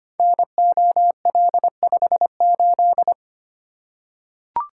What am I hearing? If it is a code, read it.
Morse code, heard as a keyed tone, about 25 wpm, 700 Hz. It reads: NOL58
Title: Code Groups - 5 Letter:Numbers - 25wpm